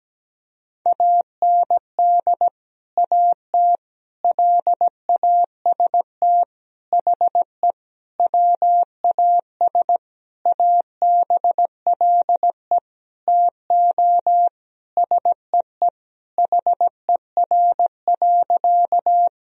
Morse code, heard as a keyed tone, about 17 wpm, 700 Hz. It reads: AND AT LAST HE WAS ABLE TO SEE HER.
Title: Sentences from Top 1000 Words - Review Part 2 - 17wpm